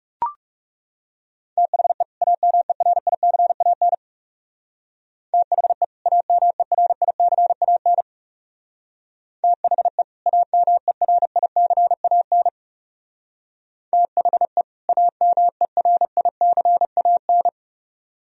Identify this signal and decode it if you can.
Morse code, heard as a keyed tone, about 37 wpm, 700 Hz. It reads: THE AMERICAN THE AMERICAN THE AMERICAN THE AMERICAN
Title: Speed-Racing - Sets of 2 Words 30wpm